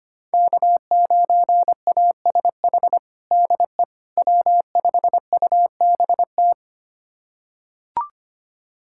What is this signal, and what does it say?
Morse code, heard as a keyed tone, about 25 wpm, 700 Hz. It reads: K9ASH DE W5UBT
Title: USA Callsign DE Callsign - 5 Characters - 25wpm